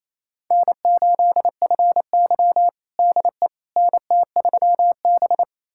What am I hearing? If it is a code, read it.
Morse code, heard as a keyed tone, about 28 wpm, 700 Hz. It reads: N8FY DE NT3B
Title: USA Callsign DE Callsign - 4 Characters - 28wpm